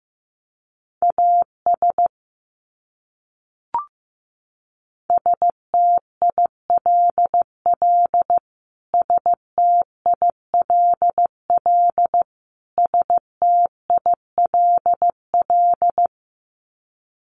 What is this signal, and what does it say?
Morse code, heard as a keyed tone, about 15 wpm, 700 Hz. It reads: AS STILL STILL STILL
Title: Top 300 Words Repeated 15wpm